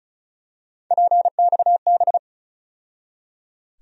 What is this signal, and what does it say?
Morse code, heard as a keyed tone, about 34 wpm, 700 Hz. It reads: PXB
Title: USA Call Sign Postfixes - 3 Letters 35wpm